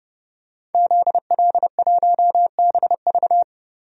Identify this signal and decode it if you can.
Morse code, heard as a keyed tone, about 30 wpm, 700 Hz. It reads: ZL1BV
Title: DXCC-100 Call Signs - 5 Characters - 30wpm